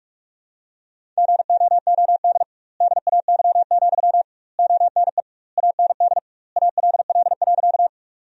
Morse code, heard as a keyed tone, about 46 wpm, 700 Hz. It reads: GOOD DAY, ONE AND ALL.
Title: 2x Word-Spacing - Sentences from Top 200 Words - 45wpm